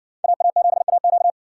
Morse code, heard as a keyed tone, about 60 wpm, 700 Hz. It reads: AA7AQ